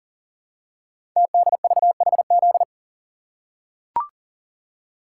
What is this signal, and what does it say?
Morse code, heard as a keyed tone, about 38 wpm, 700 Hz. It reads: TDVHZ
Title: Code Groups - 5 Letters - 40wpm